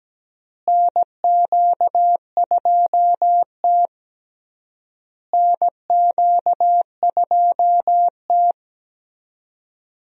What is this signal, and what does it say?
Morse code, heard as a keyed tone, about 17 wpm, 700 Hz. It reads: NQ2T NQ2T
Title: USA Call Signs - 4 Characters - Repeated 3x - 17wpm